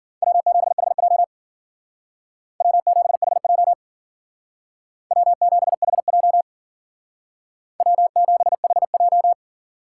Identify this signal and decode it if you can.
Morse code, heard as a keyed tone, about 51 wpm, 700 Hz. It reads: W7HJ W7HJ W7HJ W7HJ